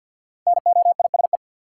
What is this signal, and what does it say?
Morse code, heard as a keyed tone, about 53 wpm, 700 Hz. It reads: NOISE
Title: Five Letter Words - 50wpm